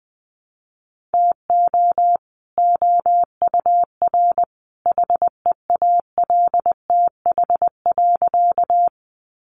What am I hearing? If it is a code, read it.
Morse code, heard as a keyed tone, about 20 wpm, 700 Hz. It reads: TO OUR HEALTH.
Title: Sentences from Top 600 Words 20wpm